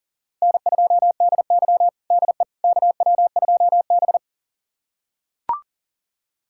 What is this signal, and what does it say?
Morse code, heard as a keyed tone, about 40 wpm, 700 Hz. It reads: N2DY DE KW2B
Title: USA Callsign DE Callsign - 4 Characters - 40wpm